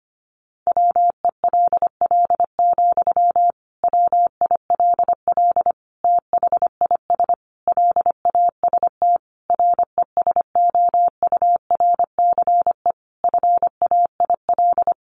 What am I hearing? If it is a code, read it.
Morse code, heard as a keyed tone, about 25 wpm, 700 Hz. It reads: WELL, WILL THIS LAST RESOURCE FAIL
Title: Sentences from Top 1300 Words 25wpm